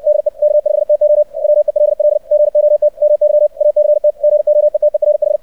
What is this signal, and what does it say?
Morse code, heard as a keyed tone, about 64 wpm, 600 Hz. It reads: DE K5TO WELL GOT MY ACT GOETERH
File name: archived / CRSnet